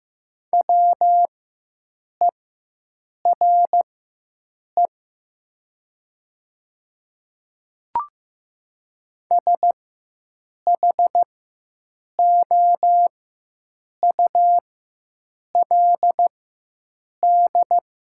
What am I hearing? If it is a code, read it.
Morse code, heard as a keyed tone, about 15 wpm, spaced as for 7 wpm, 700 Hz. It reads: WERE SHOULD
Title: 4x Character Spacing - Top 300 Words - 15wpm